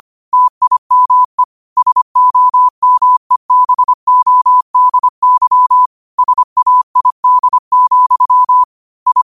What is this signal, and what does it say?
Morse code, heard as a keyed tone, about 25 wpm, 1000 Hz. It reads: TIME SOMEBODY SAID, I